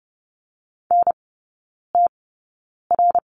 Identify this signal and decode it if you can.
Morse code, heard as a keyed tone, about 30 wpm, spaced as for 9 wpm, 700 Hz. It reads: NTR
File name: Lesson 19e - Letter w - Three Characters 30wpm